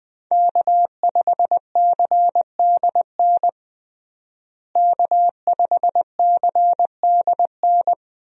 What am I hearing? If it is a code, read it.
Morse code, heard as a keyed tone, about 20 wpm, 700 Hz. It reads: K5CDN K5CDN
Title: USA Call Signs - 5 Characters - Repeated 3x - 20wpm